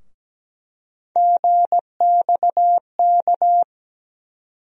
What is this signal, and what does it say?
Morse code, heard as a keyed tone, about 17 wpm, 700 Hz. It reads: GXK
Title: USA Call Sign Postfixes - 3 Letters 17wpm